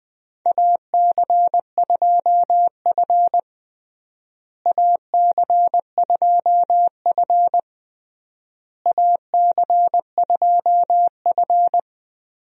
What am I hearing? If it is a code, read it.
Morse code, heard as a keyed tone, about 20 wpm, 700 Hz. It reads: AC2F AC2F AC2F
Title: USA Call Signs - 4 Characters - Repeated 3x - 20wpm